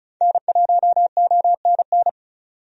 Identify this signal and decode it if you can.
Morse code, heard as a keyed tone, about 36 wpm, 700 Hz. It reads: N1ONN
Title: USA Call Signs - 5 Characters - 35wpm